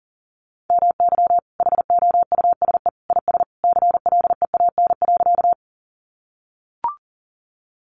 Morse code, heard as a keyed tone, about 40 wpm, 700 Hz. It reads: MY HOUSE IS CLEAN.
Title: Sentences from Top 1100 Words 40wpm